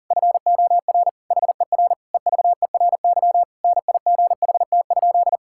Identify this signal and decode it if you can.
Morse code, heard as a keyed tone, about 40 wpm, 700 Hz. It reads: FOR HER EVERY NIGHT?
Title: Sentences from Top 1000 Words - Review Part 5 - 40wpm